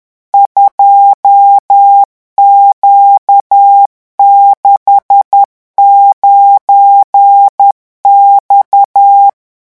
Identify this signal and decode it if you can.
Morse code, heard as a keyed tone, about 10 wpm, 800 Hz. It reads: 2Q69X
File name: morse mp3 / revision13-vitesse 10 mots